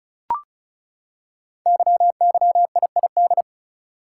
Morse code, heard as a keyed tone, about 36 wpm, 690 Hz. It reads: YYIID
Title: Code Groups - 5 Letter:Numbers - 35wpm